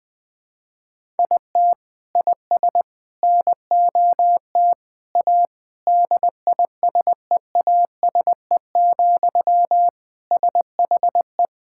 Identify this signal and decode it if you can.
Morse code, heard as a keyed tone, about 20 wpm, 700 Hz. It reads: IT IS NOT A DISEASE, SHE
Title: Sentences from Top 1000 Words - Review Part 6 - 20wpm